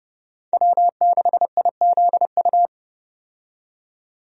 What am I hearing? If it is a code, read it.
Morse code, heard as a keyed tone, about 30 wpm, 700 Hz. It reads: W6IZU
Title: USA Call Signs - 5 Characters - 30wpm